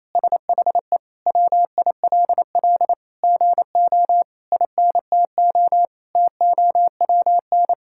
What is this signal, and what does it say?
Morse code, heard as a keyed tone, about 28 wpm, 700 Hz. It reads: SHE WILL GO INTO TOWN
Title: Sentences from Top 1000 Words - Review Part 6 - 28wpm